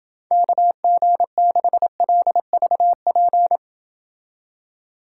Morse code, heard as a keyed tone, about 29 wpm, 700 Hz. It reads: KG6LVP